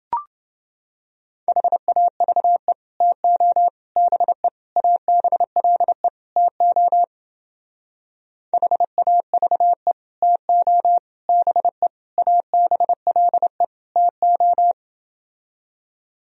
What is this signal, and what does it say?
Morse code, heard as a keyed tone, about 29 wpm, 700 Hz. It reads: HAVE TO BE ABLE TO HAVE TO BE ABLE TO